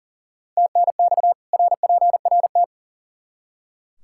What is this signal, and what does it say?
Morse code, heard as a keyed tone, about 40 wpm, 700 Hz. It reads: TNX RPRT
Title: Practice QSO - Exchange 2 - Info - 40wpm